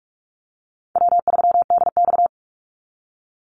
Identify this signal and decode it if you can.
Morse code, heard as a keyed tone, about 44 wpm, 700 Hz. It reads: W3DX
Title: NA SOTA Activator Callsigns 45wpm